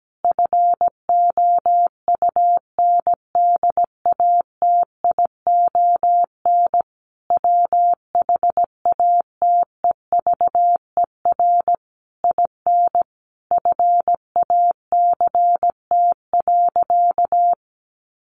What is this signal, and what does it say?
Morse code, heard as a keyed tone, about 17 wpm, 700 Hz. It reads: FOUNDATION WHATEVER IN FACT.
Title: Sentences from Top 1200 Words 17wpm